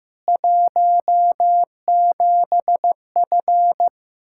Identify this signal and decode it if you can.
Morse code, heard as a keyed tone, about 15 wpm, 700 Hz. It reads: 17F
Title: Practice QSO - Exchange 2 - Info - 15wpm